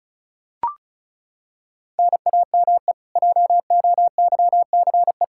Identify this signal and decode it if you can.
Morse code, heard as a keyed tone, about 36 wpm, 700 Hz. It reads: NAME JOYCE
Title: Practice QSO - Exchange 1 - Beginning - 35wpm